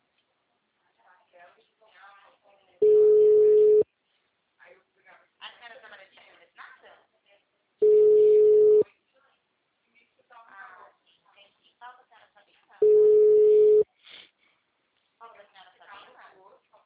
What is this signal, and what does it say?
Ringback tone (Europe, 425 Hz), with speech.